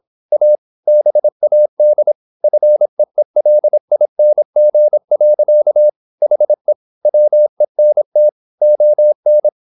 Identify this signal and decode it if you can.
Morse code, heard as a keyed tone, about 26 wpm, 600 Hz. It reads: A BAD FEELING. HE WENT ON
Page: Never 26wpm | CW med Gnister